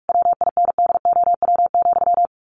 Morse code, heard as a keyed tone, about 44 wpm, 700 Hz. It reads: WINDOW,
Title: A Sound Of Thunder 45wpm